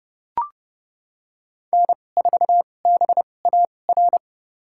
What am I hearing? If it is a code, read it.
Morse code, heard as a keyed tone, about 30 wpm, spaced as for 22 wpm, 700 Hz. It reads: N4BAR
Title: USA Call Signs - 5 Characters - 2x Character Spacing - 30wpm